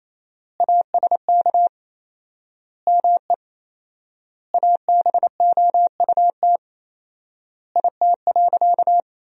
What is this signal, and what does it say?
Morse code, heard as a keyed tone, about 28 wpm, 700 Hz. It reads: ASK ME ABOUT IT.
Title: 4x Word-Spacing - Sentences from Top 200 Words - 28wpm